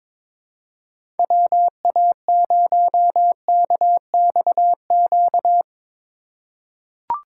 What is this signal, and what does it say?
Morse code, heard as a keyed tone, about 22 wpm, 700 Hz. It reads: WA0KXQ
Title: USA Call Signs - 3x Character Spacing - 22wpm